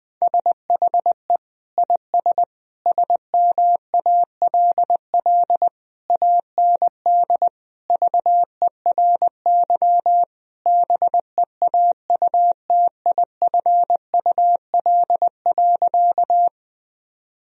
Morse code, heard as a keyed tone, about 21 wpm, 700 Hz. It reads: SHE IS SMALL AND VERY BEAUTIFUL.
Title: Sentences from Top 1000 Words - Review Part 1 - 20wpm